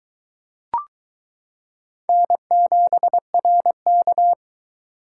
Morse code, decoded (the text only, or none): N7RK